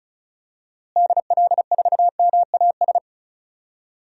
Morse code, heard as a keyed tone, about 36 wpm, 700 Hz. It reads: DL4MAS
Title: DXCC-100 Call Signs - 6 Characters - Repeated 3x - 35wpm